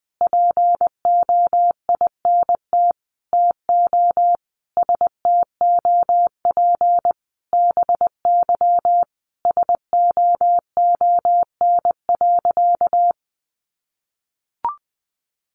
Morse code, heard as a keyed tone, about 20 wpm, 700 Hz. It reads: POINT TO STOP BY SOON.
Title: Sentences from Top 1500 Words 20wpm